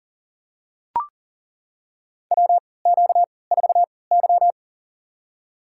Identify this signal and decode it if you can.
Morse code, heard as a keyed tone, about 40 wpm, spaced as for 23 wpm, 700 Hz. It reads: WQ4Y
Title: USA Call Signs - 4 Characters - 3x Character Spacing - 40wpm